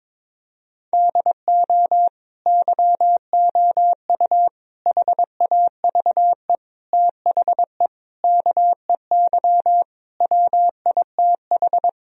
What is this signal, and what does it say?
Morse code, heard as a keyed tone, about 22 wpm, 700 Hz. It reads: DO YOU HAVE THE KEY WITH
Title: Sentences from Top 1000 Words - Review Part 4 - 22wpm